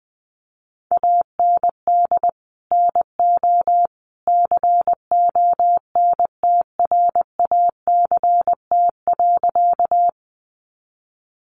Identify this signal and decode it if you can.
Morse code, heard as a keyed tone, about 20 wpm, 700 Hz. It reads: AND NO CONTRACT.
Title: Sentences from Top 1400 Words 20wpm